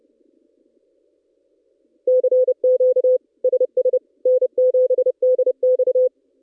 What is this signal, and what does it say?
Morse code, heard as a keyed tone, about 29 wpm, 510 Hz. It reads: CQ SS N7DX
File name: contests / ss-cw-k5tr-op-n5rz -1700